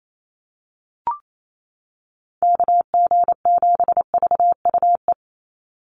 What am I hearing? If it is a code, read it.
Morse code, heard as a keyed tone, about 28 wpm, 700 Hz. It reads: KG7VUE
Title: USA Call Signs 28wpm